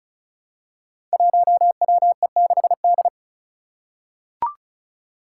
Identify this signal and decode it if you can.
Morse code, heard as a keyed tone, about 36 wpm, 700 Hz. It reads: 1WE6D
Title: Five Challenging Letter-Number Combinations 35wpm